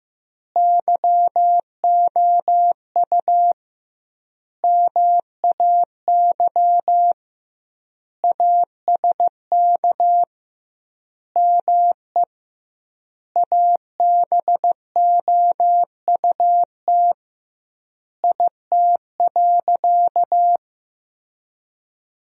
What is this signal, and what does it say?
Morse code, heard as a keyed tone, about 15 wpm, 700 Hz. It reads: YOU MAY ASK ME ABOUT IT.
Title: 2x Word-Spacing - Sentences from Top 200 Words - 15wpm